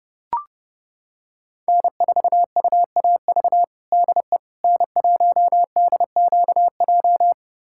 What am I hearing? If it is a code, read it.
Morse code, heard as a keyed tone, about 30 wpm, 700 Hz. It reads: N4UAV DE N1DQJ
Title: USA Callsign DE Callsign - 5 Characters - 30wpm